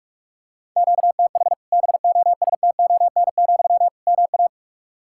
Morse code, heard as a keyed tone, about 46 wpm, 700 Hz. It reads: QTH BOSTON, MA
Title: Practice QSO - Exchange 1 - Beginning - 45wpm